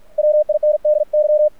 Morse code, heard as a keyed tone, about 48 wpm, 600 Hz. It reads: WINDY
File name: archived / CRSnet